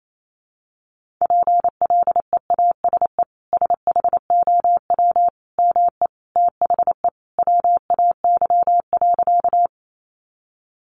Morse code, heard as a keyed tone, about 28 wpm, 700 Hz. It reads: PLEASE SHOW ME THE WAY.
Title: Sentences from Top 1500 Words 28wpm